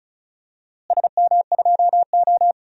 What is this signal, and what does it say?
Morse code, heard as a keyed tone, about 36 wpm, 700 Hz. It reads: SM2O
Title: DXCC-100 Call Signs - 4 Characters - Repeated 3x - 35wpm